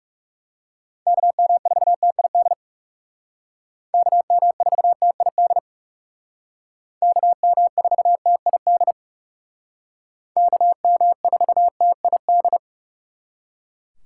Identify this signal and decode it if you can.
Morse code, heard as a keyed tone, about 37 wpm, 700 Hz. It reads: KM4TID KM4TID KM4TID KM4TID
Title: Speed Racing - USA 2x3 Call Signs 30wpm